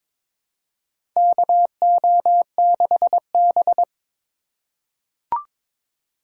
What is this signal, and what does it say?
Morse code, heard as a keyed tone, about 22 wpm, 700 Hz. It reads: KO6B
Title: USA Call Signs - 2x Character Spacing - 22wpm